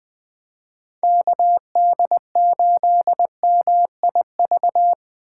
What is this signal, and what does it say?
Morse code, heard as a keyed tone, about 20 wpm, 700 Hz. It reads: KD8MIV
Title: 2023 POTA Hunter Call Signs - 20wpm